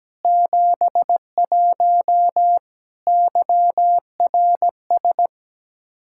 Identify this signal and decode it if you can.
Morse code, heard as a keyed tone, about 17 wpm, 700 Hz. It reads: 71 YRS
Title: Practice QSO - Exchange 3 - More Info - 17wpm